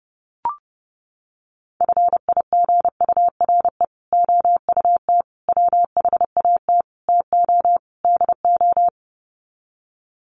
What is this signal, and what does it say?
Morse code, heard as a keyed tone, about 29 wpm, 700 Hz. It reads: FIGURE OUT WHAT TO DO
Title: Sets of 5 Words Encore2 30wpm